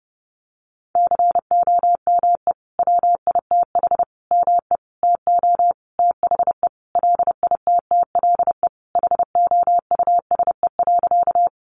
Morse code, heard as a keyed tone, about 30 wpm, 700 Hz. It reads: COME WITH ME TO THE LITTLE HOUSE.
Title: Sentences from Top 200 Words 30wpm